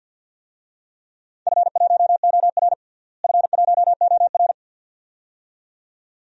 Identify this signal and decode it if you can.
Morse code, heard as a keyed tone, about 53 wpm, 700 Hz. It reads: F1OR F1OR
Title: DXCC-100 Call Signs - 4 Characters - Repeated 2x - 50wpm